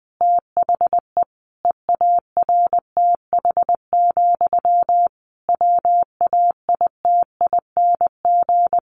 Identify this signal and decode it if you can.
Morse code, heard as a keyed tone, about 20 wpm, 700 Hz. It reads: THE EARTH, WAITING
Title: A Sound Of Thunder 20wpm